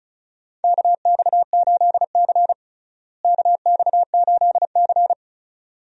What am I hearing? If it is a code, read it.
Morse code, heard as a keyed tone, about 36 wpm, 700 Hz. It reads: KX8C KX8C
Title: USA Call Signs - 4 Characters - Repeated 2x - 35wpm